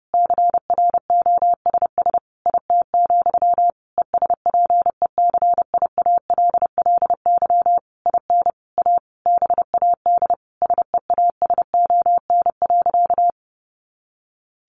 Morse code, heard as a keyed tone, about 30 wpm, 700 Hz. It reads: CROSS IT, ESPECIALLY IN A BAD SEASON.
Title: Sentences from Top 1400 Words 30wpm